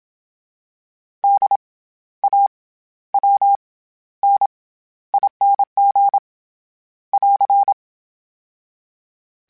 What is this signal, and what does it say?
Morse code, heard as a keyed tone, about 27 wpm, 800 Hz. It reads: D A W N ING <AR>
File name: Special INGs-mp3s / Special INGs-mp3s / SlowToFaster ing @25